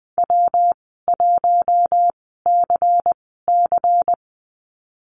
Morse code, heard as a keyed tone, about 20 wpm, spaced as for 14 wpm, 700 Hz. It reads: W1CC
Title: 238 Lesson 38g - Letter q - Call Signs 20wpm